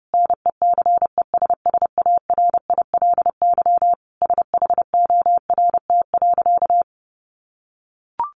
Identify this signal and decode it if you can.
Morse code, heard as a keyed tone, about 30 wpm, 700 Hz. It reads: NECESSARILY SHORT.
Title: Sentences from Top 1400 Words 30wpm